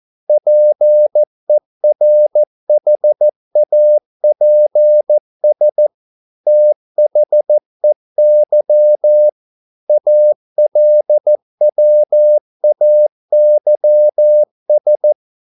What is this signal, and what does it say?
Morse code, heard as a keyed tone, about 14 wpm, 600 Hz. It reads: PERHAPS THEY ALWAYS